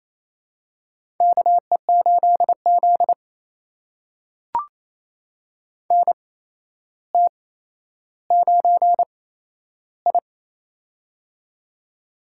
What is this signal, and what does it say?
Morse code, heard as a keyed tone, about 28 wpm, spaced as for 12 wpm, 700 Hz. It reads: KE8Z NT9I
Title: USA Call Signs - 4 Characters - 8x Character Spacing - 28wpm